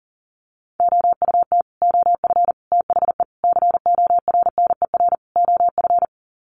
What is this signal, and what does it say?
Morse code, heard as a keyed tone, about 40 wpm, 700 Hz. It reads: OUT OF THE CORNER OF
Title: Sets of 5 Words 40wpm